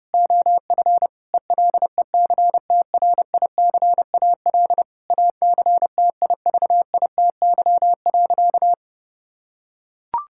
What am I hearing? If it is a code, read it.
Morse code, heard as a keyed tone, about 32 wpm, 700 Hz. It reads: OF ELECTRICAL ACTIVITY.
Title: Sentences from Top 1000 Words - Review Part 6 - 30wpm